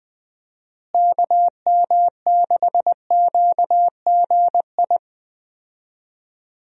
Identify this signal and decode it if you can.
Morse code, heard as a keyed tone, about 21 wpm, 700 Hz. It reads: KM6QGI
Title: USA Call Signs - 6 Characters - Repeated 2x - 20wpm